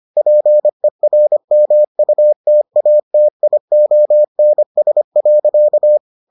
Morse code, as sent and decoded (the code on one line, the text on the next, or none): .--. .-. -- ..- - .- - .. --- -. ... .-.-.-
PERMUTATIONS.